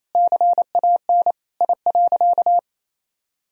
Morse code, heard as a keyed tone, about 28 wpm, 700 Hz. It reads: CAN I.
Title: Sentences from Top 1000 Words - Review Part 2 - 28wpm